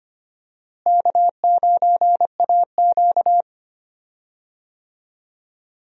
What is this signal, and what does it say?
Morse code, heard as a keyed tone, about 25 wpm, 700 Hz. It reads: K9AQ
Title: USA Call Signs - 4 Characters - Repeated 2x - 25wpm